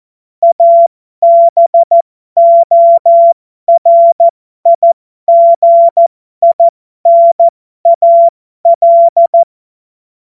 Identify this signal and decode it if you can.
Morse code, heard as a keyed tone, about 14 wpm, spaced as for 12 wpm, 680 Hz. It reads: ABORIGINAL